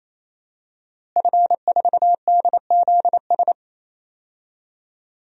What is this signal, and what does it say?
Morse code, heard as a keyed tone, about 28 wpm, 700 Hz. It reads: F4DZS
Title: DXCC-100 Call Signs - 5 Characters - Repeated 3x - 28wpm